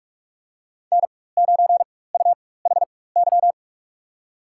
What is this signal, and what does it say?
Morse code, heard as a keyed tone, about 45 wpm, spaced as for 21 wpm, 700 Hz. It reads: N9UHY
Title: USA Call Signs - 5 Characters - 4x Character Spacing - 45wpm